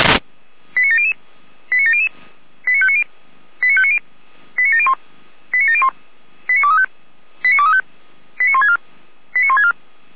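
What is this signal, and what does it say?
Five-tone selective call (ZVEI), heard as quick runs of five tones, tones of about 70 ms. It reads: 89700 89700 89590 89590 89701 89701 89235 89235 89164 89164
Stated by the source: ZVEI five-tone selective call 89092 received on an SDR.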